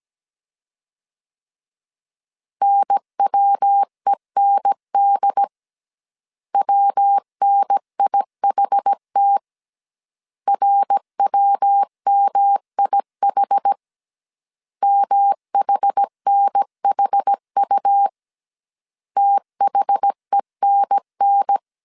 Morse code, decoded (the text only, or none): NWEND WNIHT RWMIH MHNHU THENN